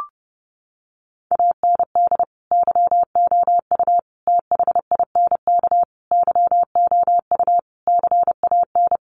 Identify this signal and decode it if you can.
Morse code, heard as a keyed tone, about 30 wpm, 700 Hz. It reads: AND YOU THINK YOU CAN
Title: Sentences from Top 1500 Words 30wpm